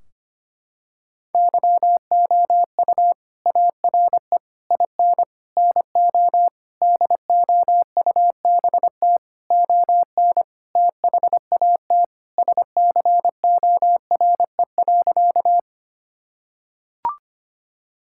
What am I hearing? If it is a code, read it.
Morse code, heard as a keyed tone, about 25 wpm, 700 Hz. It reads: YOU ARE IN NO DOUBT ON THAT SCORE.
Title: Sentences from Top 1600 Words 25wpm